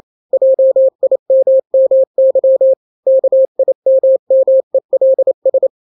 Morse code, heard as a keyed tone, about 28 wpm, 530 Hz. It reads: JIMMY KIMMELS
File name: daily / 27wpm